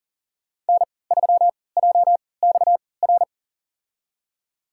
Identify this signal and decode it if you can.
Morse code, heard as a keyed tone, about 40 wpm, spaced as for 23 wpm, 700 Hz. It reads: N3JXR
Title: USA Call Signs - 5 Characters - 3x Character Spacing - 40wpm